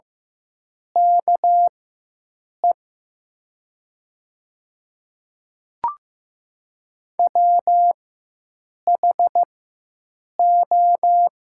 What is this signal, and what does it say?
Morse code, heard as a keyed tone, about 15 wpm, spaced as for 7 wpm, 700 Hz. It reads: KE WHO
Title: 4x Character Spacing - Top 100 Words - 15wpm